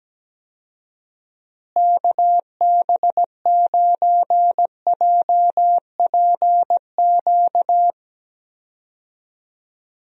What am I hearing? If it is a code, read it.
Morse code, heard as a keyed tone, about 17 wpm, 700 Hz. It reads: KB9JPQ
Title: USA Call Signs - 6 Characters - Repeated 3x - 17wpm